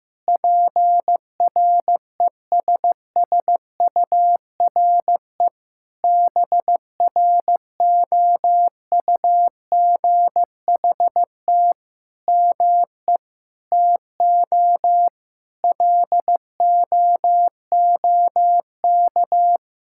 Morse code, heard as a keyed tone, about 16 wpm, 700 Hz. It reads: PRESSURE BROUGHT ME TO LOOK
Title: Sentences from Top 1000 Words - Review Part 6 - 15wpm